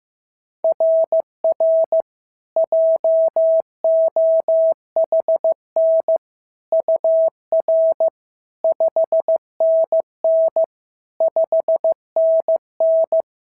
Morse code, heard as a keyed tone, about 15 wpm, 690 Hz. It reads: RR JOHN UR 5NN 5NN
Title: SOTA QSO Practice 15wpm